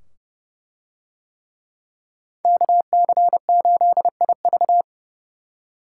Morse code, heard as a keyed tone, about 29 wpm, 700 Hz. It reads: KC8IV